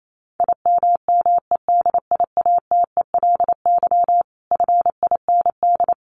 Morse code, heard as a keyed tone, about 28 wpm, 700 Hz. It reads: IMMEDIATELY FIND
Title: Sentences from Top 1200 Words 28wpm